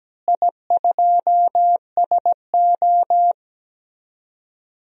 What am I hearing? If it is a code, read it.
Morse code, heard as a keyed tone, about 17 wpm, 700 Hz. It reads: I2SO
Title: DXCC-100 Call Signs - 4 Characters - Repeated 3x - 17wpm